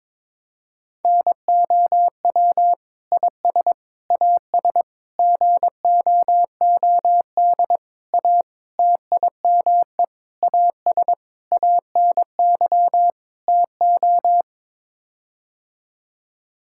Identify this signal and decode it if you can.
Morse code, heard as a keyed tone, about 22 wpm, 700 Hz. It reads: NOW IS AS GOOD A TIME AS ANY TO
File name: Sets of 9 Words - 22wpm